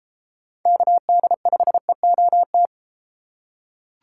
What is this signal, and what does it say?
Morse code, heard as a keyed tone, about 30 wpm, 700 Hz. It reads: KD5EOT